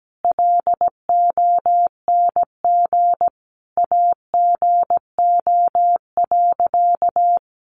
Morse code, heard as a keyed tone, about 17 wpm, 700 Hz. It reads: LONG AGO.
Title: Sentences from Top 500 Words 17wpm